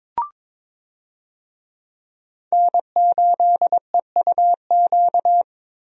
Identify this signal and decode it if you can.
Morse code, heard as a keyed tone, about 22 wpm, 700 Hz. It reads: N8EUQ
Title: USA Call Signs - 5 Characters - 22wpm